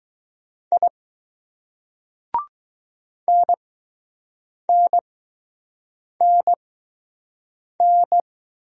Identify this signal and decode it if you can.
Morse code, heard as a keyed tone, about 20 wpm, spaced as for 6 wpm, 700 Hz. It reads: I NNNN